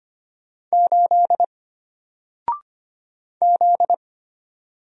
Morse code, heard as a keyed tone, about 25 wpm, 700 Hz. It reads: 8 Z
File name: Single-Letter-Number-25wpm